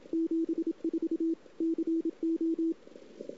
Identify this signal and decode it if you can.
Morse code, heard as a keyed tone, about 27 wpm, 330 Hz. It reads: 74 CO
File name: CW / 2008-ss-cw-k5tr-op-n5rz -02:00